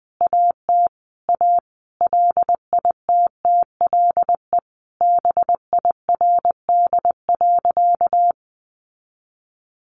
Morse code, heard as a keyed tone, about 20 wpm, 700 Hz. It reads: AT A LITTLE BIRD.
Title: Sentences from Top 1100 Words 20wpm